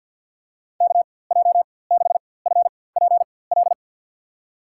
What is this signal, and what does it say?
Morse code, heard as a keyed tone, about 52 wpm, spaced as for 23 wpm, 700 Hz. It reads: KJ6FPL
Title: USA Call Signs - 6 Characters - 4x Character Spacing - 50wpm